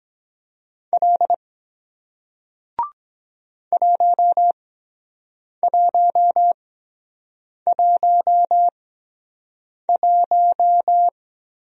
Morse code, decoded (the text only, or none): L 1111